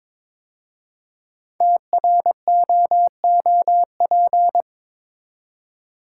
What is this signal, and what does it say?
Morse code, heard as a keyed tone, about 23 wpm, 700 Hz. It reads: TROOP
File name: 5 Letter Words - First 3 Letters Given - 22wpm